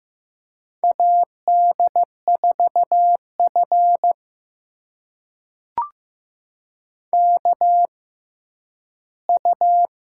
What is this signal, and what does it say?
Morse code, heard as a keyed tone, about 15 wpm, spaced as for 10 wpm, 700 Hz. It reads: AD4F KU
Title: USA Call Signs - 4 Characters - 6x Character Spacing - 15wpm